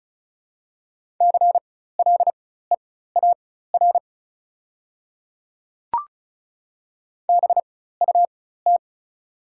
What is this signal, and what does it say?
Morse code, heard as a keyed tone, about 34 wpm, spaced as for 16 wpm, 690 Hz. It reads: CLEAR BUT
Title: 4x Character Spacing - Top 500 Words - 35wpm